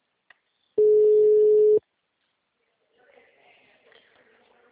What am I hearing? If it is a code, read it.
Ringback tone (Europe, 425 Hz).
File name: out- -2012- - - .30686